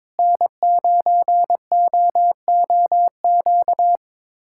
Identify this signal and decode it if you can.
Morse code, heard as a keyed tone, about 22 wpm, 700 Hz. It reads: N9OOQ
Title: USA Callsign DE Callsign 22wpm